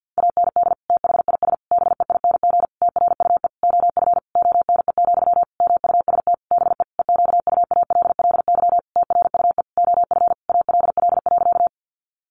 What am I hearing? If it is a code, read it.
Morse code, heard as a keyed tone, about 50 wpm, 700 Hz. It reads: AND THIS BEING TRUE OF ONE, MUST BE EQUALLY TRUE OF ALL.
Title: Sentences from Top 1400 Words 50wpm